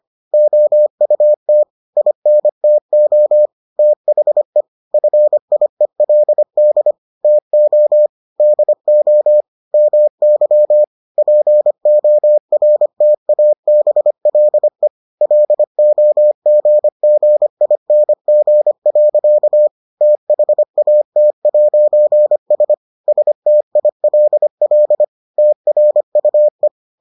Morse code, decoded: OUT INTO THE FIELD TO DO MY PORTABLE LOGGING. THAT'S STILL TRUE

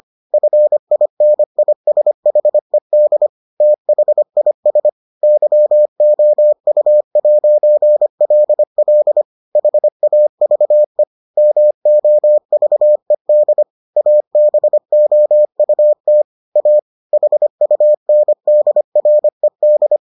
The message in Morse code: FINISHED THIS YOU'LL HAVE MOVED ABOUT A HUNDRED

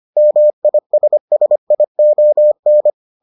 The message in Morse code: MISSION

